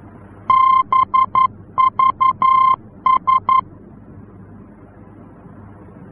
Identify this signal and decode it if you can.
Morse code, heard as a keyed tone, about 11 wpm, 1050 Hz. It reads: BVS
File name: france / mx / BVS-F-VOR